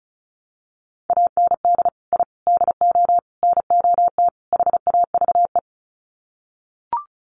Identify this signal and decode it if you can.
Morse code, heard as a keyed tone, about 34 wpm, 700 Hz. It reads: AND I DO NOT HAVE
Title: Sets of 5 Words Encore 35wpm